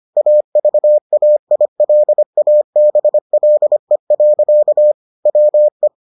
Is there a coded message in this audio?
Yes: Morse code from 0.2 to 5.9 s, heard as a keyed tone, about 25 wpm, 600 Hz. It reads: AVAILABLE. WE